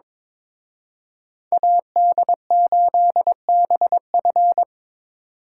Morse code, heard as a keyed tone, about 24 wpm, 700 Hz. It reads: AD8BF
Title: Speed Racing - USA 2x2 Call Signs 22wpm